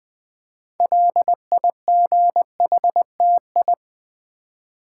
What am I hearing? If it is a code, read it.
Morse code, heard as a keyed tone, about 20 wpm, 700 Hz. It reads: LIGHTI
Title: Build A Word - 2 Letters At A Time - 8-Letter Words - 20wpm